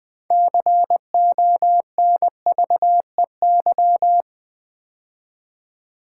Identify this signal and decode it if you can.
Morse code, heard as a keyed tone, about 20 wpm, 700 Hz. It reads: CONVEY
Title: Build A Word - 3 Letters At A Time - 6-Letter Words - 20wpm